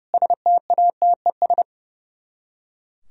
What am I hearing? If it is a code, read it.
Morse code, heard as a keyed tone, about 36 wpm, 700 Hz. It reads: STATES
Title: Speed-Racing - Sets of 5 Words 30wpm